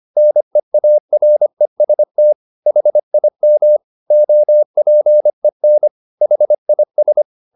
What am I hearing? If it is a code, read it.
Morse code, heard as a keyed tone, about 25 wpm, 600 Hz. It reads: NEAREST HIM OPEN HIS